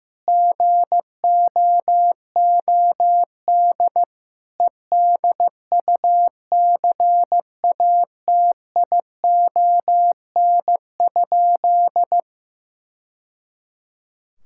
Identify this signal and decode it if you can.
Morse code, heard as a keyed tone, about 16 wpm, 700 Hz. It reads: GOOD EDUCATION?
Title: Sentences from Top 1000 Words - Review Part 6 - 15wpm